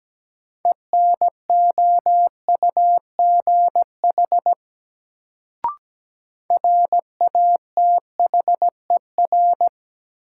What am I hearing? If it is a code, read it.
Morse code, heard as a keyed tone, about 17 wpm, 700 Hz. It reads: ENOUGH RATHER
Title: Sets of 2 Words - New Word Drill - 17wpm